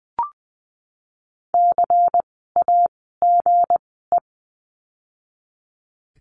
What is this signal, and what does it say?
Morse code, heard as a keyed tone, about 20 wpm, spaced as for 14 wpm, 700 Hz. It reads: CAGE
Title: 160 Lesson 26f - Letter g - Words 20wpm